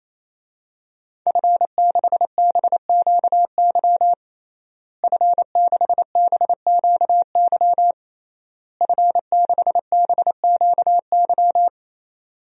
Morse code, heard as a keyed tone, about 28 wpm, 700 Hz. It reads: F6BQY F6BQY F6BQY
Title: DXCC-100 Call Signs - 5 Characters - Repeated 3x - 28wpm